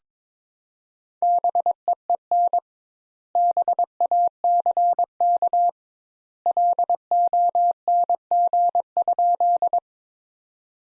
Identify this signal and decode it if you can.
Morse code, heard as a keyed tone, about 21 wpm, 700 Hz. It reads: BEEN BACK LONG?
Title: Sentences Top 200 Words - 22wpm